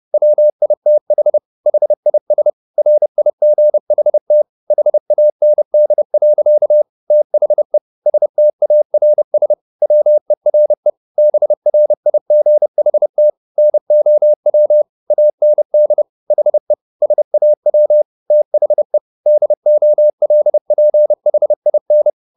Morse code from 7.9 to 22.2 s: STARS WERE BRIGHT NOW AND HE SAW THE DOLPHIN

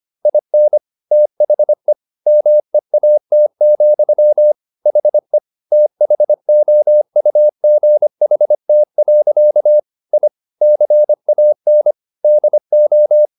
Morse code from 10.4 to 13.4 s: CAN DO

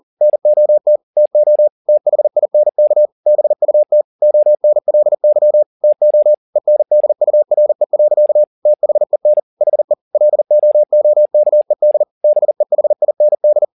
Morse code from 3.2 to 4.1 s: BUT

